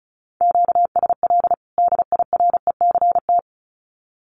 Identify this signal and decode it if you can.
Morse code, heard as a keyed tone, about 34 wpm, 700 Hz. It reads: QSL DIRECT
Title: Common QSO Phrases 35wpm